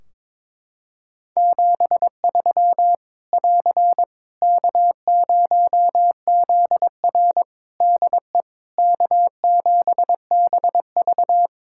Morse code, heard as a keyed tone, about 22 wpm, 700 Hz. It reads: 73 <AR> K0ZR DE K7BV
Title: K1USN SST QSO Practice 22wpm